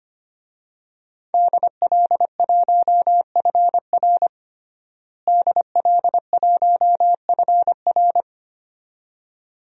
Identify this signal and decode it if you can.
Morse code, heard as a keyed tone, about 25 wpm, 700 Hz. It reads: DL1FR DL1FR
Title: DXCC-100 Call Signs - 5 Characters - Repeated 2x - 25wpm